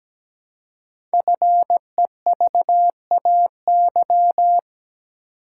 Morse code, heard as a keyed tone, about 17 wpm, 700 Hz. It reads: FEVAY
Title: Code Groups - 5 Letter:Numbers - 17wpm